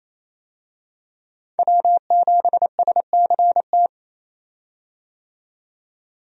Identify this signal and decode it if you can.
Morse code, heard as a keyed tone, about 28 wpm, 700 Hz. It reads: W7SCT
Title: USA Call Signs - 5 Characters - 28wpm